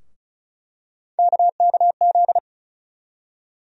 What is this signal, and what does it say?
Morse code, heard as a keyed tone, about 36 wpm, 700 Hz. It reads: KKZ
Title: USA Call Sign Postfixes - 3 Letters 35wpm